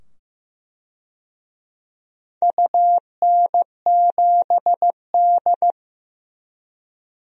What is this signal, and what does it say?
Morse code, heard as a keyed tone, about 15 wpm, 700 Hz. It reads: UN7D
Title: DXCC-100 Call Signs - 4 Characters - 15wpm